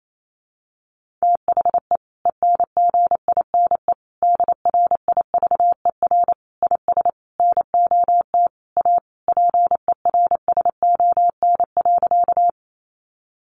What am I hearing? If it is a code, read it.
Morse code, heard as a keyed tone, about 28 wpm, 700 Hz. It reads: THE ENGINE DRIVER IS NOT A PERSON.
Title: Sentences from Top 1500 Words 28wpm